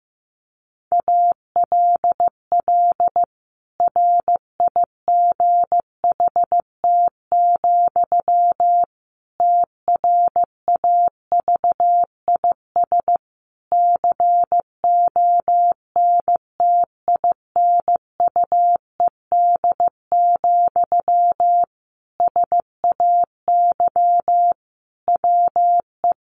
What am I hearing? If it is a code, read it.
Morse code, heard as a keyed tone, about 15 wpm, 700 Hz. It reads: ALL RIGHT, TRAVIS CONTINUED, SAY WE
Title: A Sound Of Thunder 15wpm